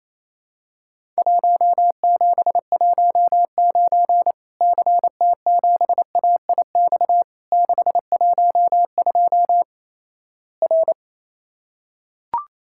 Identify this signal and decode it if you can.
Morse code, heard as a keyed tone, about 28 wpm, 700 Hz. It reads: 1719 CT7AIX 612 R
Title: Worked All Europe DX Contest - QTC Exchange - 28wpm